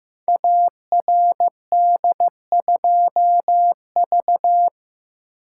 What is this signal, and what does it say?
Morse code, heard as a keyed tone, about 15 wpm, 700 Hz. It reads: ARD2V
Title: Code Groups - 5 Letter:Numbers - 15wpm